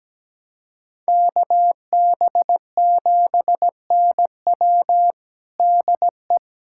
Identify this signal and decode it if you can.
Morse code, heard as a keyed tone, about 17 wpm, 700 Hz. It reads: KB7NW DE
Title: USA Callsign DE Callsign - 5 Characters - 17wpm